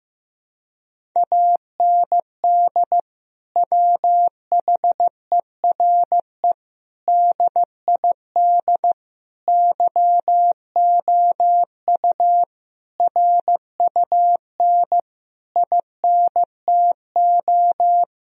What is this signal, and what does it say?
Morse code, heard as a keyed tone, about 15 wpm, 700 Hz. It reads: AND WHERE DID YOU RUN INTO
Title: Sentences from Top 1000 Words - Review Part 1 - 15wpm